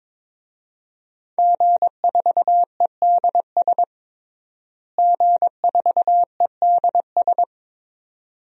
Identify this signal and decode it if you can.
Morse code, heard as a keyed tone, about 22 wpm, 700 Hz. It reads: G4EDS G4EDS
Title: DXCC-100 Call Signs - 5 Characters - Repeated 2x - 22wpm